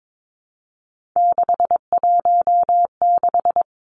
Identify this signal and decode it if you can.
Morse code, heard as a keyed tone, about 22 wpm, 700 Hz. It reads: 616
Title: Three Challenging Letter-Number Combinations 22wpm